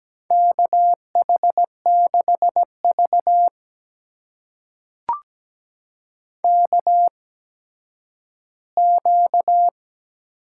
Morse code, heard as a keyed tone, about 17 wpm, spaced as for 10 wpm, 700 Hz. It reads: KH6V KQ
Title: USA Call Signs - 8x Character Spacing - 17wpm